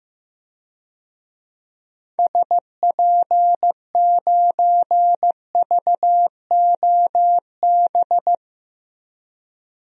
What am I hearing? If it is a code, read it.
Morse code, heard as a keyed tone, about 15 wpm, 700 Hz. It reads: SP9VOB
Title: DXCC-100 Call Signs - 6 Characters - Repeated 2x - 15wpm